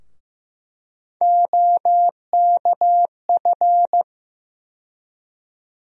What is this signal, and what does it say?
Morse code, heard as a keyed tone, about 15 wpm, 700 Hz. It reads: OKF
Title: USA Call Sign Postfixes - 3 Letters 15wpm